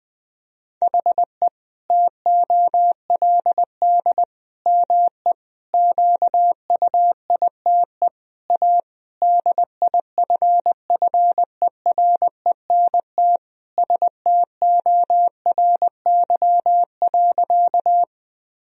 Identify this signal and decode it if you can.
Morse code, heard as a keyed tone, about 21 wpm, 700 Hz. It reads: HE TOLD ME QUITE A DIFFERENT STORY.
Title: Sentences from Top 1000 Words - Review Part 3 - 20wpm